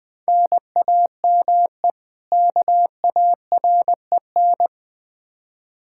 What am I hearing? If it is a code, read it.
Morse code, heard as a keyed tone, about 21 wpm, 700 Hz. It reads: NAME KAREN
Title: Practice QSO - Exchange 1 - Beginning - 20wpm